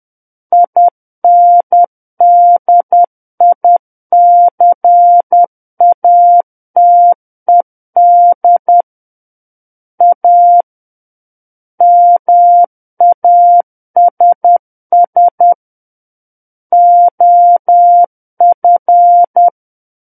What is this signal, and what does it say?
Morse code, heard as a keyed tone, about 10 wpm, 700 Hz. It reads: INDICATED A MASS OF
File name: Book One / 01-Chapter 1 - 10 WPM